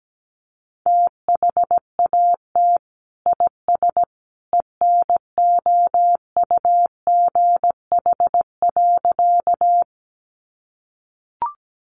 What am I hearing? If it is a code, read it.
Morse code, heard as a keyed tone, about 17 wpm, 700 Hz. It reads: THAT IS ENOUGH.
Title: Sentences from Top 400 Words 17wpm